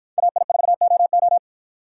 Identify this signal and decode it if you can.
Morse code, heard as a keyed tone, about 51 wpm, 700 Hz. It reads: AI4OO